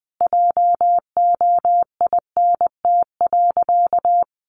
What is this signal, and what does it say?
Morse code, heard as a keyed tone, about 20 wpm, 700 Hz. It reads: JOINT.
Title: Sentences from Top 1300 Words 20wpm